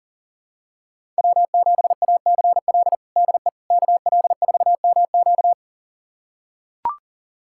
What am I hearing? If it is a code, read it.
Morse code, heard as a keyed tone, about 40 wpm, 700 Hz. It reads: W7ACL DE KL4MQ
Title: USA Callsign DE Callsign - 5 Characters - 40wpm